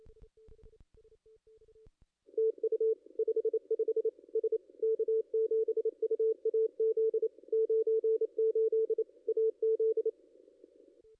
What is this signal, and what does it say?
Morse code, heard as a keyed tone, about 31 wpm, 440 Hz. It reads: TU 55 S K7UAZ 98 AZ
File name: contests / ss-cw-k5tr-op-n5rz -2200